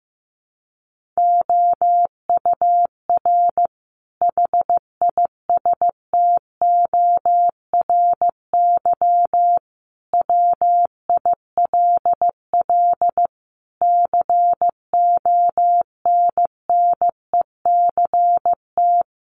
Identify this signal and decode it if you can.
Morse code, heard as a keyed tone, about 15 wpm, 700 Hz. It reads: OUR HISTORY WILL CONNECT
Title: Sentences from Top 1500 Words 15wpm